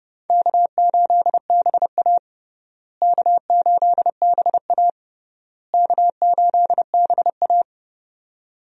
Morse code, heard as a keyed tone, about 30 wpm, 700 Hz. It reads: K8BA K8BA K8BA
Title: USA Call Signs - 4 Characters - Repeated 3x - 30wpm